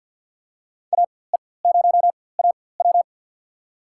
Morse code, heard as a keyed tone, about 53 wpm, spaced as for 24 wpm, 700 Hz. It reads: AE0AW
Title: USA Call Signs - 5 Characters - 4x Character Spacing - 50wpm